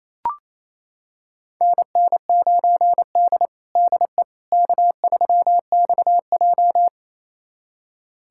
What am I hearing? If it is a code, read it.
Morse code, heard as a keyed tone, about 28 wpm, 700 Hz. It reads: NN9D DE K3XJ
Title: USA Callsign DE Callsign - 4 Characters - 28wpm